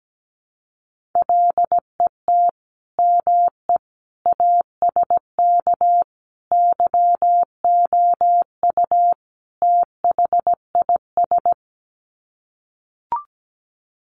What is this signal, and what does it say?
Morse code, heard as a keyed tone, about 17 wpm, 700 Hz. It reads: LET ME ASK YOU THIS
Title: Sets of 5 Words 17wpm